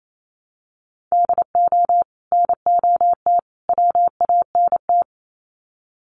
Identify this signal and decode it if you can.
Morse code, heard as a keyed tone, about 28 wpm, 700 Hz. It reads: DO NOT WANT
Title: Sets of 3 Words 28wpm